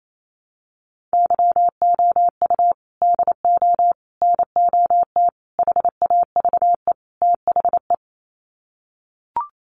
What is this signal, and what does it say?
Morse code, heard as a keyed tone, about 28 wpm, 700 Hz. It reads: YOU DO NOT HAVE THE
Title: Sets of 5 Words 28wpm